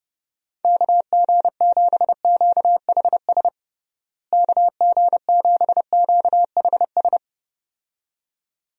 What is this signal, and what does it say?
Morse code, heard as a keyed tone, about 30 wpm, 700 Hz. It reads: KG7QHS KG7QHS
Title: USA Call Signs - 6 Characters - Repeated 2x - 30wpm